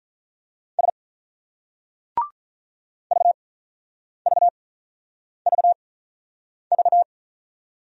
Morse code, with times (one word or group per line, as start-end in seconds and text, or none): 0.8-0.9 S
3.1-7.0 VVVV